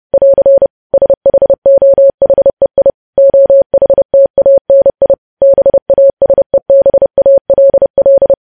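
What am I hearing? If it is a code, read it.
Morse code, heard as a keyed tone, about 30 wpm, 550 Hz. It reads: <AR> SHOHEI OHTANI BASEBALL
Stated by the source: Morse code sending of news headlines at 30 wpm.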